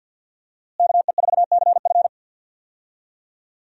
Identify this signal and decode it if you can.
Morse code, heard as a keyed tone, about 53 wpm, 700 Hz. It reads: KE4CF
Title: USA Call Signs - 5 Characters - Repeated 2x - 50wpm